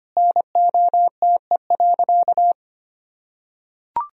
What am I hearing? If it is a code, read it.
Morse code, heard as a keyed tone, about 25 wpm, 700 Hz. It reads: NOTE.
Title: Sentences from Top 1000 Words - Review Part 2 - 25wpm